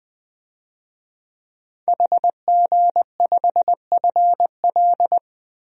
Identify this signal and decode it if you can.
Morse code, heard as a keyed tone, about 20 wpm, 700 Hz. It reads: HG5FL
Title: DXCC-100 Call Signs - 5 Characters - 20wpm